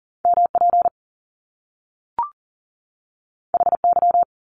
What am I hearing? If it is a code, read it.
Morse code, heard as a keyed tone, about 39 wpm, 700 Hz. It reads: MP HY
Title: Top Two Letter Combinations in Words 40wpm